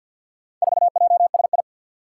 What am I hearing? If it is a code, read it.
Morse code, heard as a keyed tone, about 47 wpm, 700 Hz. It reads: 4JSI
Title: Four Challenging Letter-Number Combinations 50wpm